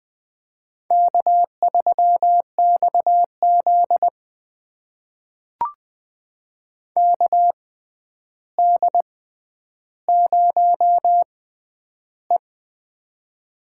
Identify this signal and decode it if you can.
Morse code, heard as a keyed tone, about 20 wpm, spaced as for 10 wpm, 700 Hz. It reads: K3XZ KD0E
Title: USA Call Signs - 4 Characters - 6x Character Spacing - 20wpm